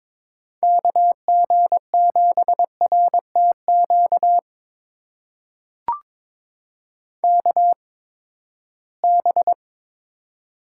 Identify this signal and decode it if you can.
Morse code, heard as a keyed tone, about 22 wpm, spaced as for 15 wpm, 700 Hz. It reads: KG7RTQ KB
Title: USA Call Signs - 6 Characters - 8x Character Spacing - 22wpm